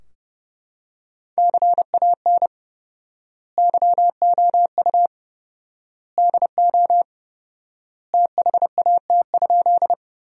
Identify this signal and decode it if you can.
Morse code, heard as a keyed tone, about 30 wpm, 700 Hz. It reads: CAN YOU DO THAT?
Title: 4x Word-Spacing - Sentences from Top 100 Words - 30wpm